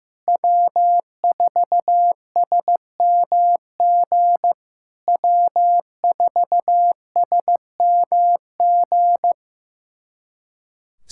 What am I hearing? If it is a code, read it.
Morse code, heard as a keyed tone, about 15 wpm, 700 Hz. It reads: W4SMG W4SMG
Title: SOTA QSO Practice 15wpm